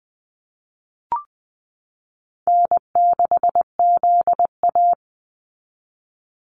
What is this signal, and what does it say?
Morse code, heard as a keyed tone, about 20 wpm, 700 Hz. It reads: N6ZA
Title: NA SOTA Activator Callsigns 20wpm